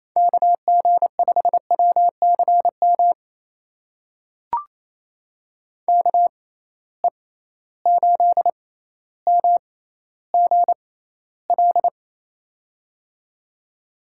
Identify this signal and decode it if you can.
Morse code, heard as a keyed tone, about 28 wpm, spaced as for 14 wpm, 700 Hz. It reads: KG5WCM KE8MGL
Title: USA Call Signs - 6 Characters - 6x Character Spacing - 28wpm